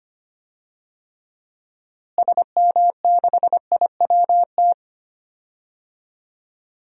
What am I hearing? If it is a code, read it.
Morse code, heard as a keyed tone, about 25 wpm, 700 Hz. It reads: SM6IWT
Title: DXCC-100 Call Signs - 6 Characters - Repeated 2x - 25wpm